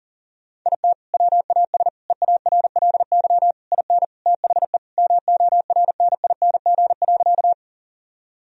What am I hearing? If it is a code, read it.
Morse code, heard as a keyed tone, about 37 wpm, 700 Hz. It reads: IT WAS EARLY IN THE MORNING.
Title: Sentences from Top 1000 Words - Review Part 2 - 40wpm